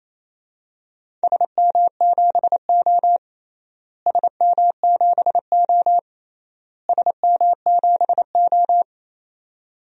Morse code, heard as a keyed tone, about 28 wpm, 700 Hz. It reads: SM7O SM7O SM7O
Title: DXCC-100 Call Signs - 4 Characters - Repeated 3x - 28wpm